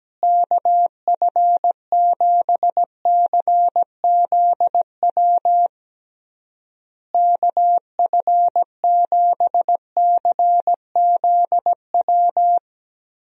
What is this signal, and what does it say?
Morse code, heard as a keyed tone, about 17 wpm, 700 Hz. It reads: KF7CZW KF7CZW
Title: USA Call Signs - 6 Characters - Repeated 3x - 17wpm